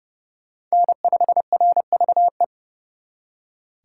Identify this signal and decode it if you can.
Morse code, heard as a keyed tone, about 34 wpm, 700 Hz. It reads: N5RVE
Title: Speed Racing - USA 1x3 Call Signs 30wpm